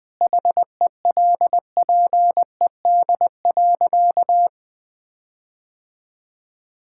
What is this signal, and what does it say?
Morse code, heard as a keyed tone, about 20 wpm, 700 Hz. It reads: HELPED.
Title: Sentences with 4-Word Phrases - 20wpm